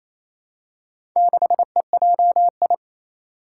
Morse code, heard as a keyed tone, about 29 wpm, 700 Hz. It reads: 6EJI
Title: Four Challenging Letter-Number Combinations 28wpm